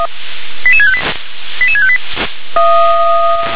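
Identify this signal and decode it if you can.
Five-tone selective call (ZVEI), heard as quick runs of five tones, tones of about 70 ms. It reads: 88658 88658
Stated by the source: ZVEI five-tone call 88768 received by SDR.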